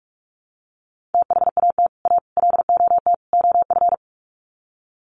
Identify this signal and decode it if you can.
Morse code, heard as a keyed tone, about 44 wpm, 700 Hz. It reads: THAT A LOT OF
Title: Sets of 4 Words 45wpm